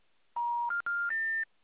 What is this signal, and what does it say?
Special information tone.
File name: out- -2044- - - .53145